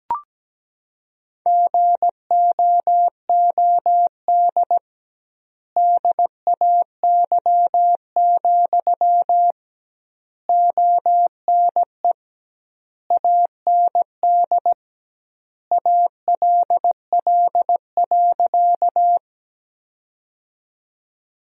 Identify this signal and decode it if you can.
Morse code, heard as a keyed tone, about 17 wpm, 700 Hz. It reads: GOOD DAY, ONE AND ALL.
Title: 2x Word-Spacing - Sentences from Top 200 Words - 17wpm